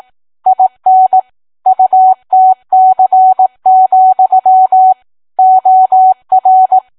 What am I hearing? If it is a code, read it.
Morse code, heard as a keyed tone, about 18 wpm, 750 Hz. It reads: IN UTC, OR